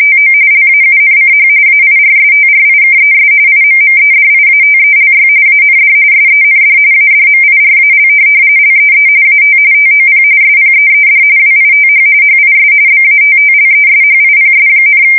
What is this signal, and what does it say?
RTTY, about 45.45 baud. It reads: TO LAUNCH AIRSTRIKES IMMEDIATELY AND SEND ADMIRAL LEE AT TOP SPEED WITH HIS BATTLESHIPS,
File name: fsk170 / USS-New-Jersey